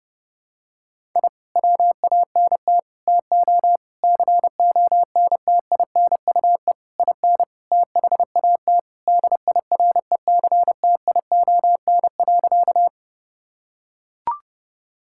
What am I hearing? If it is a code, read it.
Morse code, heard as a keyed tone, about 30 wpm, 700 Hz. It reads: I WANT TO CONTINUE IN THAT DIRECTION.
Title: Sentences from Top 1000 Words - Review Part 6 - 30wpm